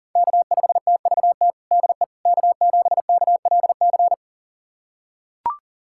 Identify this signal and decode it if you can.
Morse code, heard as a keyed tone, about 37 wpm, 700 Hz. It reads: K5TVT DE K7KLC
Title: USA Callsign DE Callsign - 5 Characters - 40wpm